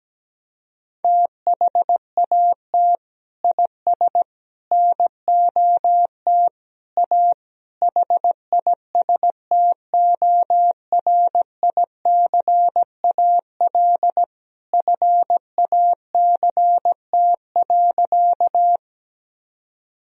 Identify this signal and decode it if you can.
Morse code, heard as a keyed tone, about 17 wpm, 700 Hz. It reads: THAT IS NOT A HISTORICAL FACT.
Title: Sentences from Top 1000 Words - Review Part 4 - 17wpm